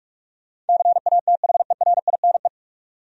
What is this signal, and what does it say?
Morse code, heard as a keyed tone, about 46 wpm, 700 Hz. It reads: CATHERINE
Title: USA Common Names - 45wpm